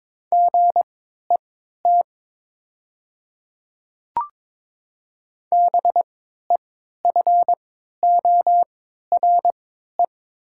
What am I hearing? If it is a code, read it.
Morse code, heard as a keyed tone, about 22 wpm, spaced as for 12 wpm, 700 Hz. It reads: GET BEFORE
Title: 3x Character Spacing - Top 300 Words - 22wpm